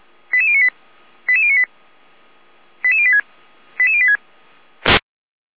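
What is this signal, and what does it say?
Five-tone selective call (ZVEI), heard as quick runs of five tones, tones of about 70 ms. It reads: 88098 88098 88086 88086
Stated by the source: ZVEI five-tone call 88571 received on an SDR.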